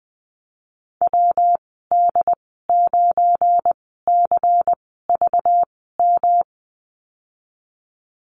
Morse code, heard as a keyed tone, about 20 wpm, spaced as for 15 wpm, 700 Hz. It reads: WD9CVM
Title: 231 Lesson 37g - Letter x - Call Signs 20wpm